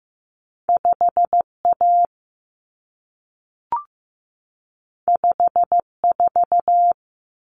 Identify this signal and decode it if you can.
Morse code, heard as a keyed tone, about 15 wpm, 700 Hz. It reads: 5A 54
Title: Common Cut Numbers - 2-Digit 15wpm